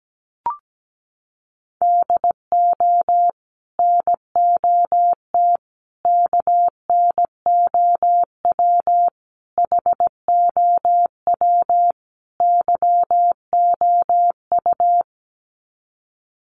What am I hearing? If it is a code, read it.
Morse code, heard as a keyed tone, about 17 wpm, 700 Hz. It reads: DO NOT KNOW HOW YOU
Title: Sets of 5 Words Encore 17wpm